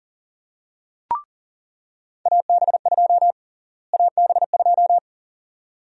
Morse code, heard as a keyed tone, about 37 wpm, 700 Hz. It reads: AB2 AB2
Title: USA Call Sign Prefixes - Repeated 2x - 40wpm